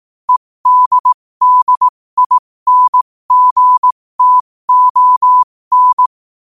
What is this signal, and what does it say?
Morse code, heard as a keyed tone, about 18 wpm, spaced as for 15 wpm, 1000 Hz. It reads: EDDINGTON
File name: qotdfiles / 15wpm